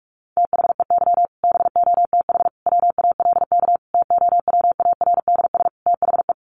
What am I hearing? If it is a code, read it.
Morse code, heard as a keyed tone, about 44 wpm, 700 Hz. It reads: THEY BOTH WALK TOWARDS THE
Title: Additional Easy Phrases 45wpm